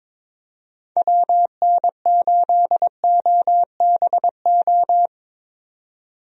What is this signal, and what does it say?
Morse code, heard as a keyed tone, about 22 wpm, 700 Hz. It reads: WN8OBO
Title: USA Call Signs - 6 Characters - 22wpm